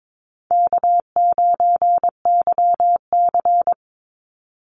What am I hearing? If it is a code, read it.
Morse code, heard as a keyed tone, about 22 wpm, 700 Hz. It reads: K9YC
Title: Easy Call Signs Repeated 3x - 22wpm